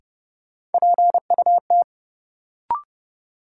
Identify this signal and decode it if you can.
Morse code, heard as a keyed tone, about 31 wpm, 700 Hz. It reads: PUT
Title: Sentences from Top 200 Words - New Word Drill - 30wpm